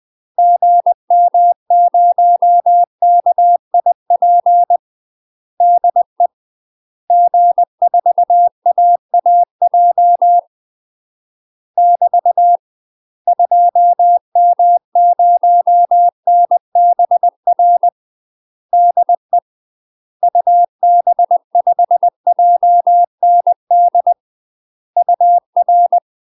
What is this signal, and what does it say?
Morse code, heard as a keyed tone, about 20 wpm, 700 Hz. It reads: GM0KIP DE G4AAJ <BT> 2M0NBR DE UB5JND UR
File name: UK MASTER QSOs TRAINING / 20WPM / UK MASTER QSOs 0014